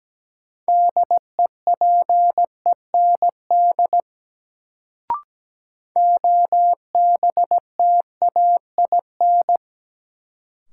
Morse code, heard as a keyed tone, about 17 wpm, 700 Hz. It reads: DEPEND OBTAIN
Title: Sentences from Top 1300 Words - New Word Drill Alt - 17wpm